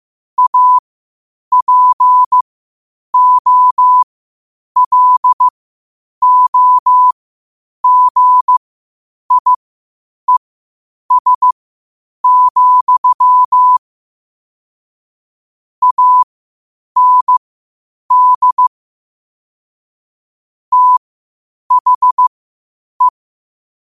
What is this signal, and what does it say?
Morse code, heard as a keyed tone, about 15 wpm, spaced as for 8 wpm, 1000 Hz. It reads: APOLOGIES, AND THE
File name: qotdfiles / 8wpm